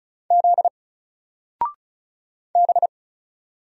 Morse code, heard as a keyed tone, about 36 wpm, 700 Hz. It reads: Z B
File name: Single-Letter-Number-35wpm